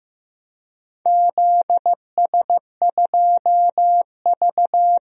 Morse code, heard as a keyed tone, about 16 wpm, 700 Hz. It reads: ZS2V
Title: DXCC-100 Call Signs - 4 Characters - 15wpm